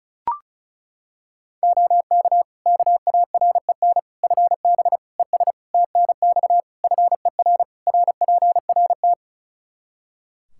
Morse code, heard as a keyed tone, about 36 wpm, 700 Hz. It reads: OK KAREN FB ES TNX FER RPRT
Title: Practice QSO - Exchange 2 - Info - 35wpm